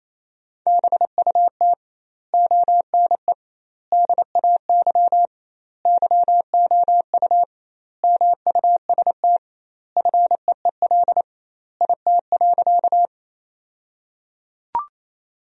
Morse code, heard as a keyed tone, about 28 wpm, 700 Hz. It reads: BUT ONE DAY YOU MUST FEEL IT.
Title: 2x Word-Spacing - Sentences from Top 400 Words - 28wpm